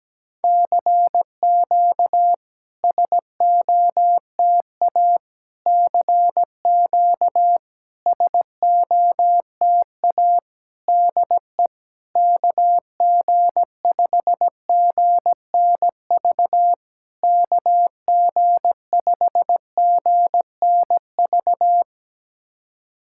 Morse code, heard as a keyed tone, about 17 wpm, 700 Hz. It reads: CQ SOTA CQ SOTA DE KG5GNV KG5GNV
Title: SOTA QSO Practice 17wpm